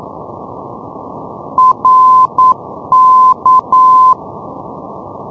Morse code, heard as a keyed tone, about 9 wpm, 1030 Hz. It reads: RK
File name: europe / isl / isl-620 ndb reykjavik rk-355-np-isl-2012-04-k-rb